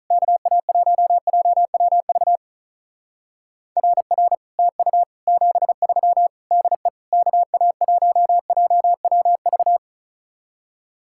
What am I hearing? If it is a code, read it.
Morse code, heard as a keyed tone, about 44 wpm, 700 Hz. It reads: KA1JWV RR TU 73 DE KA1JWV